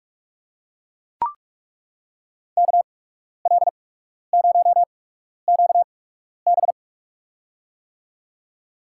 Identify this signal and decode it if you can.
Morse code, heard as a keyed tone, about 42 wpm, spaced as for 12 wpm, 690 Hz. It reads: KL0QB
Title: USA Call Signs - 5 Characters - 8x Character Spacing - 45wpm